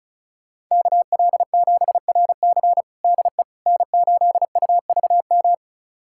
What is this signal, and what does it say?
Morse code, heard as a keyed tone, about 36 wpm, 700 Hz. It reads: KL7RC DE N8UVM
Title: USA Callsign DE Callsign - 5 Characters - 35wpm